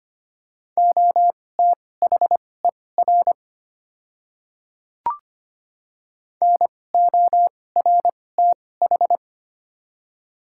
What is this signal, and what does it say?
Morse code, heard as a keyed tone, about 25 wpm, spaced as for 18 wpm, 700 Hz. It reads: OTHER NORTH
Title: 2x Character Spacing - Top 300 Words - 25wpm